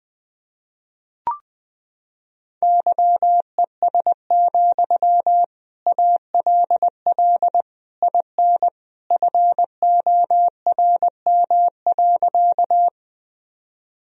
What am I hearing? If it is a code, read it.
Morse code, heard as a keyed tone, about 20 wpm, 700 Hz. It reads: YES, ALL IN FORM.
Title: Sentences from Top 1000 Words - Review Part 1 - 20wpm